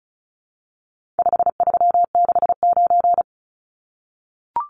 Morse code, heard as a keyed tone, about 34 wpm, 700 Hz. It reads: 5369
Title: Numbers - 4-Digit 35wpm